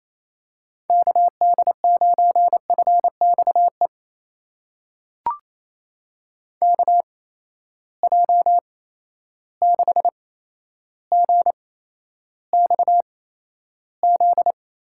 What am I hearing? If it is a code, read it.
Morse code, heard as a keyed tone, about 28 wpm, spaced as for 12 wpm, 700 Hz. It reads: KD9FXE KJ6GXZ
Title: USA Call Signs - 6 Characters - 8x Character Spacing - 28wpm